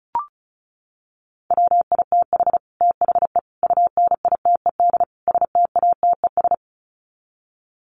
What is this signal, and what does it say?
Morse code, heard as a keyed tone, about 34 wpm, 700 Hz. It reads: WITH THE UNITED STATES
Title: Sets of 4 Words 35wpm